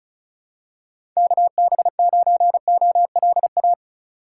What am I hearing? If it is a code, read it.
Morse code, heard as a keyed tone, about 36 wpm, 700 Hz. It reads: KB9OLA
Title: USA Call Signs - 6x Character Spacing - 35wpm